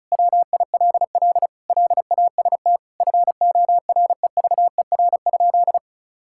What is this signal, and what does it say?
Morse code, heard as a keyed tone, about 34 wpm, 700 Hz. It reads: WILL LAST FOREVER?
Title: Sentences from Top 1600 Words 35wpm